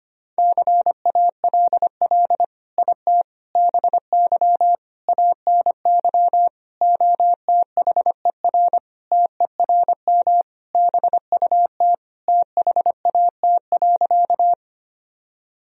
Morse code, heard as a keyed tone, about 25 wpm, 700 Hz. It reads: CALL IT BY ANY OTHER TERM BUT THAT.
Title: Sentences from Top 1000 Words - Review Part 4 - 25wpm